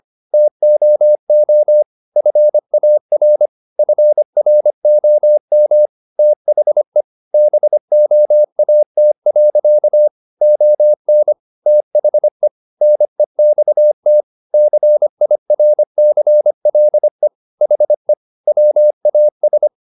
Morse code, heard as a keyed tone, about 25 wpm, 600 Hz. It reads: TOO FAR FROM THE BOAT. ON THE NEXT CIRCLE HE WAS